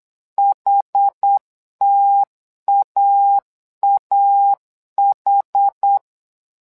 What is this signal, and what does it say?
Morse code, heard as a keyed tone, about 8 wpm, 800 Hz. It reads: HTAAH
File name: LEZIONI / 2 UHDN / 2 UHD 8wpm 2258